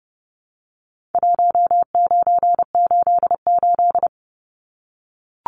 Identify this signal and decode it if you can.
Morse code, heard as a keyed tone, about 30 wpm, 700 Hz. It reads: 1988
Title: Numbers - Contest Style - 4-Digit 30wpm